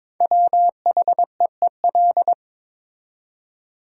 Morse code, heard as a keyed tone, about 22 wpm, 700 Hz. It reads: WHEEL
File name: 5 Letter Words - First 3 Letters Given - 22wpm